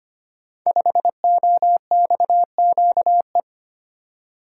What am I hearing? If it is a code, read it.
Morse code, heard as a keyed tone, about 25 wpm, 700 Hz. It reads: 5OXQE
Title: Code Groups - 5 Letter:Numbers - 25wpm